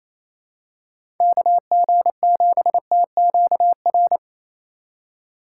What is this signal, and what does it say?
Morse code, heard as a keyed tone, about 28 wpm, 700 Hz. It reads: KG7TQR
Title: USA Call Signs - 3x Character Spacing - 28wpm